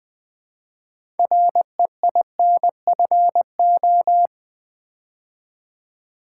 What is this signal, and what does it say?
Morse code, heard as a keyed tone, about 20 wpm, 700 Hz. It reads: REINFO
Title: Build A Word - 3 Letters At A Time - 9-Letter Words - 20wpm